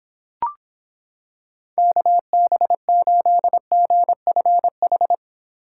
Morse code, heard as a keyed tone, about 26 wpm, 700 Hz. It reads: KB8GFH